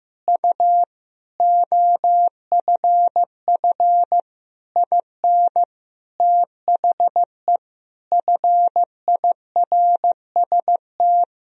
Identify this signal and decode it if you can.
Morse code, heard as a keyed tone, about 15 wpm, 700 Hz. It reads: U OFF IN THE FIRST
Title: Sentences from Top 200 Words and Abbreviations 15wpm